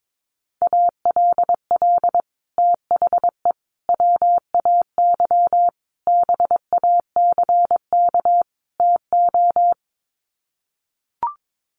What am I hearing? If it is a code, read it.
Morse code, heard as a keyed tone, about 22 wpm, 700 Hz. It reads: ALL THE WAY BACK TO
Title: Sets of 5 Words Encore 22wpm